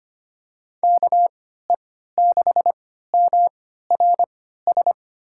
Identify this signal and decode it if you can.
Morse code, heard as a keyed tone, about 25 wpm, spaced as for 14 wpm, 700 Hz. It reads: KE6MRS
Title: USA Call Signs - 6 Characters - 3x Character Spacing - 25wpm